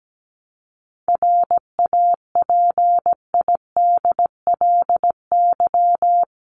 Morse code, heard as a keyed tone, about 17 wpm, 700 Hz. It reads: RAPIDLY
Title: LY Suffix - Top 100 Words with Base Words 17wpm